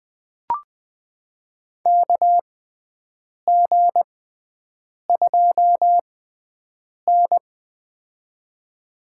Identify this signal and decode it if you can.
Morse code, heard as a keyed tone, about 20 wpm, spaced as for 7 wpm, 700 Hz. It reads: KG2N
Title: USA Call Signs - 4 Characters - 6x Character Spacing - 20wpm